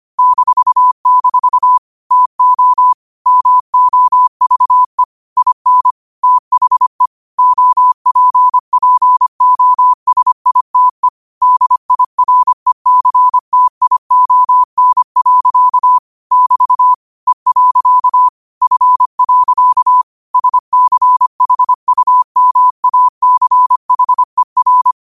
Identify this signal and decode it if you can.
Morse code, heard as a keyed tone, about 25 wpm, 1000 Hz. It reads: -- TO MOVE IN THE OPPOSITE DIRECTION. <BT> E. F. SCHUMACHER